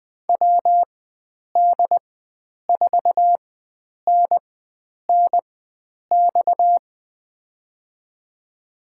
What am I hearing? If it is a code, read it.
Morse code, heard as a keyed tone, about 20 wpm, spaced as for 9 wpm, 700 Hz. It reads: WD4NNX
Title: USA Call Signs - 6 Characters - 4x Character Spacing - 20wpm